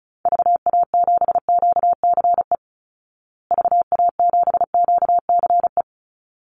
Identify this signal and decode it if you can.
Morse code, heard as a keyed tone, about 34 wpm, 700 Hz. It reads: VA7QCE VA7QCE
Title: Call Signs Repeated 3x - 35wpm